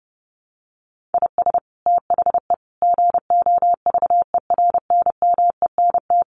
Morse code, heard as a keyed tone, about 30 wpm, 700 Hz. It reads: IS THE GOVERNMENT
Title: Sentences from Top 1500 Words 30wpm